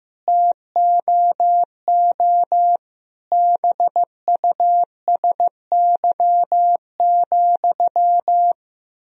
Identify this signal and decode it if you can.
Morse code, heard as a keyed tone, about 16 wpm, 700 Hz. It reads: TOO BUSY,
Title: Sentences from Top 1000 Words - Review Part 3 - 15wpm